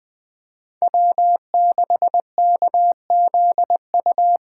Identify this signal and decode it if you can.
Morse code, heard as a keyed tone, about 20 wpm, 700 Hz. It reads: W6KZU
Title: USA Call Signs - 5 Characters - 8x Character Spacing - 20wpm